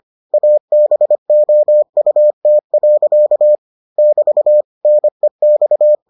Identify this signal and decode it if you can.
Morse code, heard as a keyed tone, about 25 wpm, 600 Hz. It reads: ABOUT. <BT> NEX